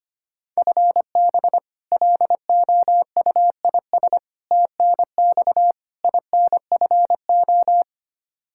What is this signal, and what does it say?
Morse code, heard as a keyed tone, about 25 wpm, 700 Hz. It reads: FB LOUIS TNX INFO
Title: Practice QSO - Exchange 3 - More Info - 25wpm